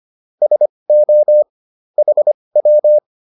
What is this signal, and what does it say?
Morse code, heard as a keyed tone, about 25 wpm, spaced as for 20 wpm, 600 Hz. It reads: SO HW
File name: QSO210a 20